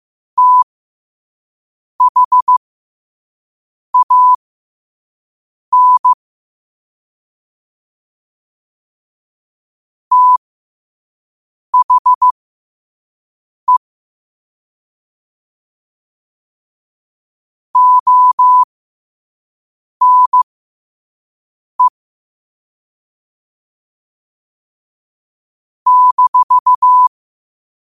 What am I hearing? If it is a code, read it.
Morse code, heard as a keyed tone, about 15 wpm, spaced as for 5 wpm, 1000 Hz. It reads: THAN THE ONE -
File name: 5wpm